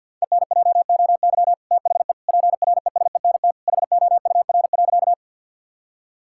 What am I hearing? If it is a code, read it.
Morse code, heard as a keyed tone, about 50 wpm, 700 Hz. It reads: ENJOY THE PRESENT HOUR.
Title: Sentences from Top 1000 Words - Review Part 5 - 50wpm